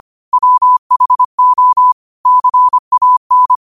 Morse code, heard as a keyed tone, about 25 wpm, 1000 Hz. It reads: WHO CAN